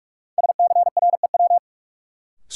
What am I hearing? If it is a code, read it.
Morse code, heard as a keyed tone, about 46 wpm, 700 Hz. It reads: SCREW
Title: Sentences from Top 1600 Words - New Word Drill Alt - 45wpm